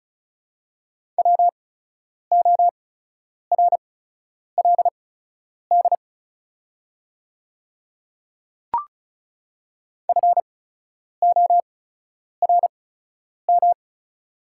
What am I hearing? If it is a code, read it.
Morse code, heard as a keyed tone, about 35 wpm, spaced as for 10 wpm, 690 Hz. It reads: WORLD FORM
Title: 8x Character Spacing - Top 200 Words - 35wpm